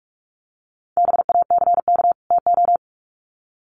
Morse code, heard as a keyed tone, about 44 wpm, 700 Hz. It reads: BACK TO
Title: Sets of 2 Words 45wpm